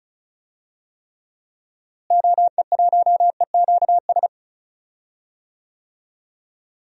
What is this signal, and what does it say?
Morse code, heard as a keyed tone, about 36 wpm, 700 Hz. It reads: OE1EQS
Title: DXCC-100 Call Signs - 6 Characters - Repeated 3x - 35wpm